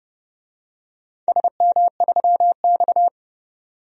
Morse code, heard as a keyed tone, about 30 wpm, 700 Hz. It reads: SM3X
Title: DXCC-100 Call Signs - 4 Characters - 30wpm